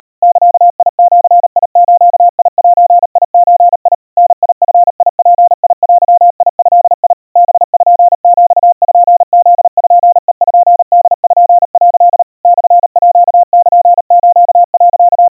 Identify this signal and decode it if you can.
Morse code (Japanese Wabun code), heard as a keyed tone, about 37 wpm, 700 Hz. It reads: ザジズゼゾ ダヂヅデド パピプペポ」 モールス、
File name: aiueo 130 cpm 700 Hz Crispy